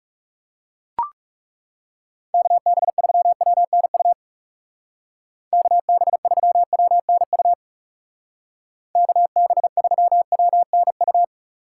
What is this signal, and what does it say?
Morse code, heard as a keyed tone, about 38 wpm, 700 Hz. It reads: KB3WNU KB3WNU KB3WNU